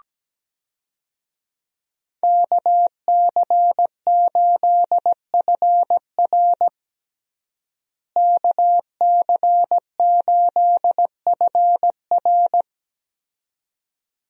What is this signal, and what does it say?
Morse code, heard as a keyed tone, about 17 wpm, 700 Hz. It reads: KC8FR KC8FR
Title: USA Call Signs - 5 Characters - Repeated 2x - 17wpm